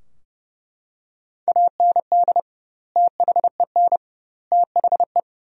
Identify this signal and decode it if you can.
Morse code, heard as a keyed tone, about 30 wpm, 700 Hz. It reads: AND THEN THE
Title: 2x Word-Spacing - Sentences from Top 500 Words - 30wpm